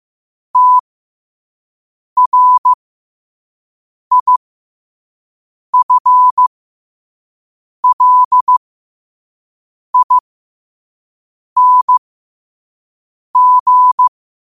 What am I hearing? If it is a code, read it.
Morse code, heard as a keyed tone, about 15 wpm, spaced as for 5 wpm, 1000 Hz. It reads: TRIFLING
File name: qotdfiles / 5wpm